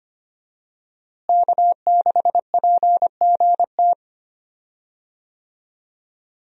Morse code, heard as a keyed tone, about 25 wpm, 700 Hz. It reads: K6PGT
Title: USA Call Signs - 5 Characters - Repeated 3x - 25wpm